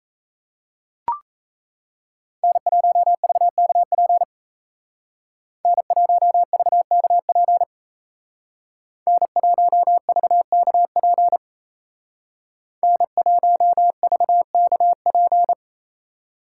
Morse code, heard as a keyed tone, about 35 wpm, 700 Hz. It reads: N1VKP N1VKP N1VKP N1VKP